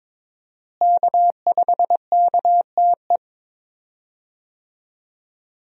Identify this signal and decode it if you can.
Morse code, heard as a keyed tone, about 22 wpm, 700 Hz. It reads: K5KTE
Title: USA Call Signs - 5 Characters - Repeated 3x - 22wpm